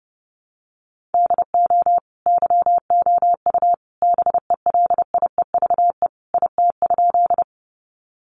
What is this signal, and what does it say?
Morse code, heard as a keyed tone, about 28 wpm, 700 Hz. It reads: DO YOU BELIEVE IT?
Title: Sentences from Top 1000 Words 30wpm